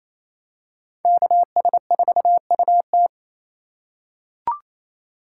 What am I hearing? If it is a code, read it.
Morse code, heard as a keyed tone, about 28 wpm, 700 Hz. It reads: KS4UT
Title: USA Call Signs - 5 Characters - 2x Character Spacing - 28wpm